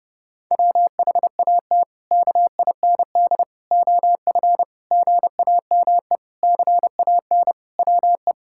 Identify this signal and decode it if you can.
Morse code, heard as a keyed tone, about 28 wpm, 700 Hz. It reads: WHAT KIND OF GAME CAN WE
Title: Sentences from Top 1000 Words - Review Part 6 - 30wpm